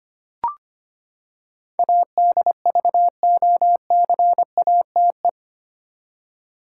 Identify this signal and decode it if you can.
Morse code, heard as a keyed tone, about 25 wpm, 700 Hz. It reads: ADVOCATE
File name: Eight Letter Words - 25wpm